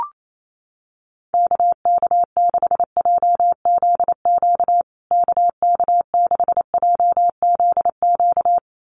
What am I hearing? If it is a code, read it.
Morse code, heard as a keyed tone, about 28 wpm, 700 Hz. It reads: KK6JZQ KK6JZQ
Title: USA Call Signs Repeated 2x 28wpm